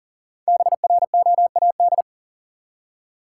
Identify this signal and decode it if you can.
Morse code, heard as a keyed tone, about 40 wpm, 700 Hz. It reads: BROAD
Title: Five Letter Words - 40wpm